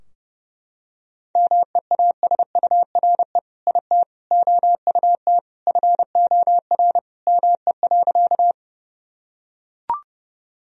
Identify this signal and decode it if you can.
Morse code, heard as a keyed tone, about 32 wpm, 700 Hz. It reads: MEASURE IT OUT FOR ME.
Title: Sentences from Top 1000 Words - Review Part 2 - 30wpm